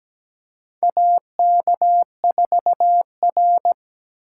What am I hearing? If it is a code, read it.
Morse code, heard as a keyed tone, about 17 wpm, 700 Hz. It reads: AK4R
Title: USA Call Signs - 4 Characters - 3x Character Spacing - 17wpm